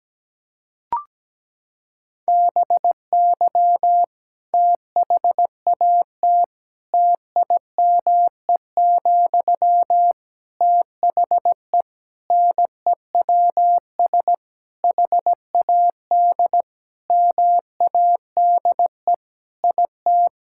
Morse code, heard as a keyed tone, about 17 wpm, 700 Hz. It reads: BY THAT TIME, THE NEWS HAD MADE IT
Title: Sentences from Top 1600 Words 17wpm